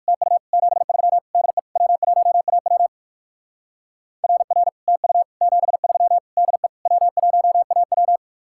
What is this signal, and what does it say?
Morse code, heard as a keyed tone, about 55 wpm, 700 Hz. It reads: TU 73 DE W1AW RR TU 73 DE W1AW
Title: Speed-Racing Common QSO Phrases - 45wpm